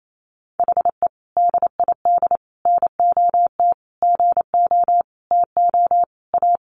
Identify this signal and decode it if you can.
Morse code, heard as a keyed tone, about 28 wpm, 700 Hz. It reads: HE DID NOT GO TO A
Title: Sentences from Top 1200 Words 28wpm